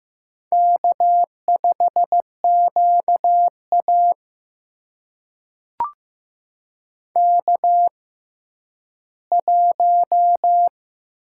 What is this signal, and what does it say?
Morse code, heard as a keyed tone, about 15 wpm, spaced as for 10 wpm, 700 Hz. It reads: K5QA K1
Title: USA Call Signs - 4 Characters - 6x Character Spacing - 15wpm